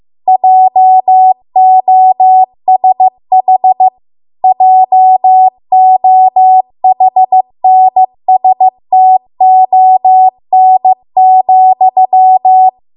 Morse code, heard as a keyed tone, about 15 wpm, 750 Hz. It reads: JOSH JOHNSTON,